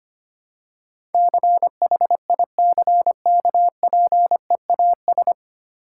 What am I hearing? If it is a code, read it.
Morse code, heard as a keyed tone, about 25 wpm, 700 Hz. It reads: CHICKPEAS
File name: practice / 25wpm / Difficult Words with Hints - 25wpm